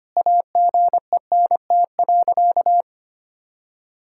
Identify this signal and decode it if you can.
Morse code, heard as a keyed tone, about 25 wpm, 700 Hz. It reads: AGENT.
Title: Sentences from Top 1600 Words 25wpm